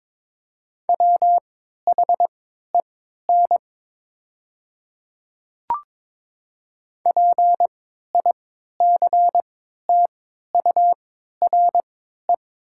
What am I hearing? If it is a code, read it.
Morse code, heard as a keyed tone, about 22 wpm, spaced as for 12 wpm, 700 Hz. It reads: WHEN PICTURE
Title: 3x Character Spacing - Top 200 Words - 22wpm